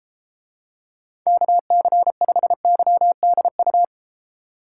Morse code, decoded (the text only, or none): KC5YDU